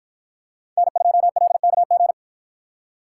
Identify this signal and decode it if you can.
Morse code, heard as a keyed tone, about 51 wpm, 700 Hz. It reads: N2LKG